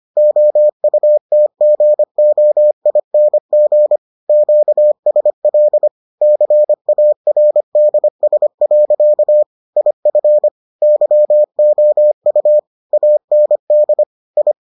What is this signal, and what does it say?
Morse code, heard as a keyed tone, about 25 wpm, 600 Hz. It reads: OUTGOING QSL CARDS. IF YOU AND I